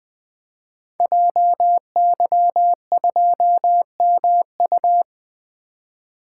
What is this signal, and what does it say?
Morse code, heard as a keyed tone, about 20 wpm, 700 Hz. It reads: JY2MU
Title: Code Groups - 5 Letter:Numbers - 20wpm